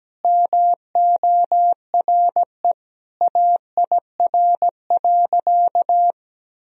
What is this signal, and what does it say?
Morse code, heard as a keyed tone, about 17 wpm, 700 Hz. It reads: MORE AIR.
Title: Sentences from Top 1000 Words - Review Part 1 - 17wpm